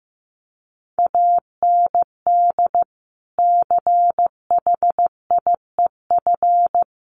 Morse code, heard as a keyed tone, about 15 wpm, 700 Hz. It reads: AND CHIEF
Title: Sets of 5 Words Encore2 15wpm